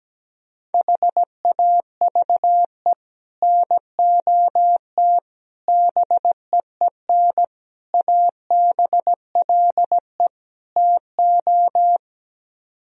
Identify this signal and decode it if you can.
Morse code, heard as a keyed tone, about 17 wpm, 700 Hz. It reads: HAVE NOT BEEN ABLE TO
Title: Speed-Racing - Sets of 5 Words 17wpm